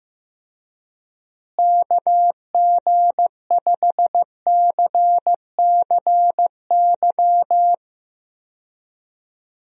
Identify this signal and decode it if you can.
Morse code, heard as a keyed tone, about 15 wpm, 700 Hz. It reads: KG5CCY
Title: USA Call Signs - 6 Characters - Repeated 2x - 15wpm